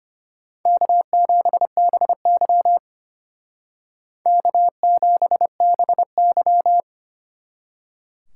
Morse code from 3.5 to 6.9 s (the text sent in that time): K7BY